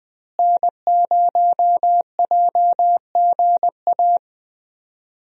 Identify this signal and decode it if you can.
Morse code, heard as a keyed tone, about 20 wpm, 700 Hz. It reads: N0JGA
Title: USA Call Signs - 5 Characters - Repeated 3x - 20wpm